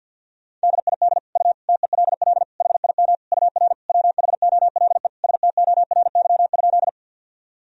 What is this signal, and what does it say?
Morse code, heard as a keyed tone, about 51 wpm, 700 Hz. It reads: DID U TELL HIM UR WHOLE STORY?
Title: Sentences from Top 400 Words and Abbreviations 50wpm